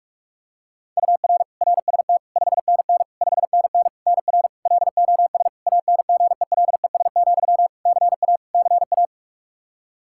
Morse code, heard as a keyed tone, about 46 wpm, 700 Hz. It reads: UR RST 5NN 5NN NR LOS ANGELES, CA CA
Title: K1USN SST QSO Practice 45wpm